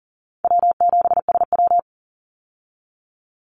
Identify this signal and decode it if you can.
Morse code, heard as a keyed tone, about 42 wpm, 700 Hz. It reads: W7SW
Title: Easy Call Signs - 40wpm